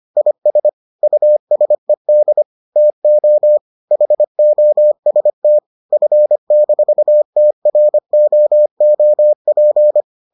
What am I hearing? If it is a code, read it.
Morse code, heard as a keyed tone, about 25 wpm, 600 Hz. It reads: IS USED TO HOST F-TROOP